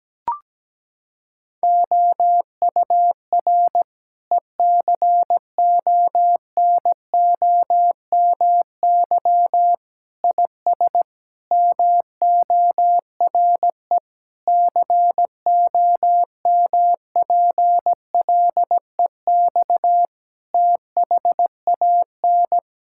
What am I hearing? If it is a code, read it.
Morse code, heard as a keyed tone, about 17 wpm, 700 Hz. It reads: OUR ECONOMY IS MORE COMPLEX THAN
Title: Sentences from Top 1000 Words - Review Part 6 - 17wpm